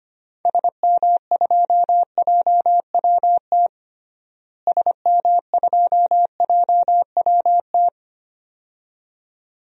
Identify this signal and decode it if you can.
Morse code, heard as a keyed tone, about 25 wpm, 700 Hz. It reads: SM2JWT SM2JWT
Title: DXCC-100 Call Signs - 6 Characters - Repeated 2x - 25wpm